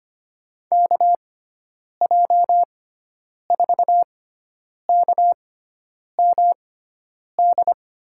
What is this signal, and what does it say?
Morse code, heard as a keyed tone, about 25 wpm, spaced as for 9 wpm, 700 Hz. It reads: KJ4KMD
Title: USA Call Signs - 6 Characters - 6x Character Spacing - 25wpm